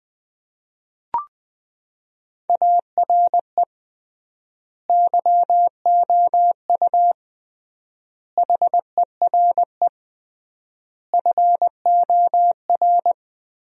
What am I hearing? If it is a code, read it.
Morse code, heard as a keyed tone, about 20 wpm, 700 Hz. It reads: ARE YOU HERE FOR
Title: 3x Word-Spacing - Sentences from Top 200 Words - 20wpm